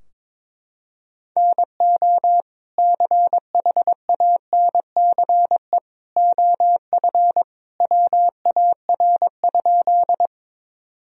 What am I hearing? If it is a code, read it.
Morse code, heard as a keyed tone, about 22 wpm, 700 Hz. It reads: NO CHANCE OF WAR?
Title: Sentences from Top 1000 Words - Review Part 2 - 22wpm